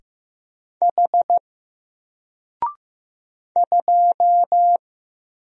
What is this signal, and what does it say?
Morse code, heard as a keyed tone, about 15 wpm, 700 Hz. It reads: H 2
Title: Single Character (Letter:Number:Punctuation:Prosign) Alt 15wpm